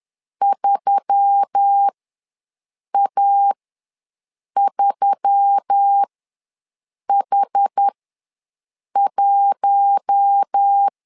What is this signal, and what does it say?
Morse code, heard as a keyed tone, about 10 wpm, spaced as for 6 wpm, 800 Hz. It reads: 3A3H1
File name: LEZIONI / 4 Q1S3 ws 10wpm 2437